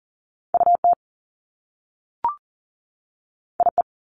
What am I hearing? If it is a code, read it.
Morse code, heard as a keyed tone, about 41 wpm, 700 Hz. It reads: UT IE
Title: Top Two Letter Combinations in Words 40wpm